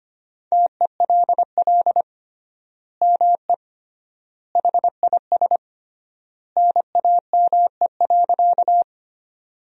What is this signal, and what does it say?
Morse code, heard as a keyed tone, about 25 wpm, 700 Hz. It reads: TELL ME HIS NAME.
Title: 3x Word-Spacing - Sentences from Top 200 Words - 25wpm